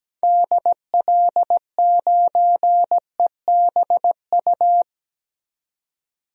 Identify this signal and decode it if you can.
Morse code, heard as a keyed tone, about 17 wpm, 700 Hz. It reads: DL9EBU
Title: DXCC-100 Call Signs - 6 Characters - 17wpm